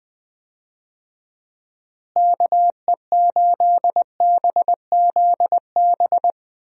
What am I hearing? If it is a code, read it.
Morse code, heard as a keyed tone, about 20 wpm, 700 Hz. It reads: KE8BZB
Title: USA Call Signs - 6 Characters - Repeated 3x - 20wpm